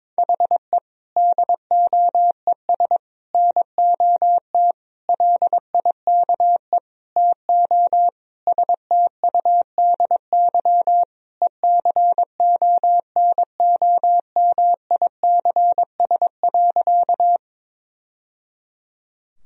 Morse code, heard as a keyed tone, about 21 wpm, 700 Hz. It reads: HE DOES NOT LIKE TO STUDY ECONOMICS.
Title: Sentences from Top 1000 Words - Review Part 3 - 22wpm